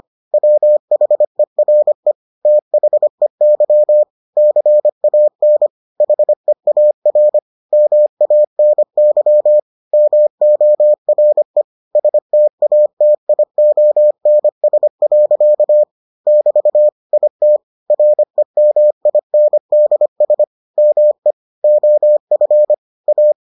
Morse code, heard as a keyed tone, about 25 wpm, 600 Hz. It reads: WHERE THEY CAN HEAR MANY MORE STATIONS. <BT> IT REMINDS ME OF A